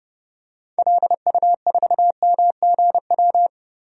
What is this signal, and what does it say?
Morse code, heard as a keyed tone, about 30 wpm, 700 Hz. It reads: LU4MGW
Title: DXCC-100 Call Signs - 6 Characters - 30wpm